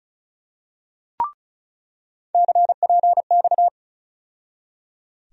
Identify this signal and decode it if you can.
Morse code, heard as a keyed tone, about 36 wpm, 700 Hz. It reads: CPX
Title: USA Call Sign Postfixes - 3 Letters 35wpm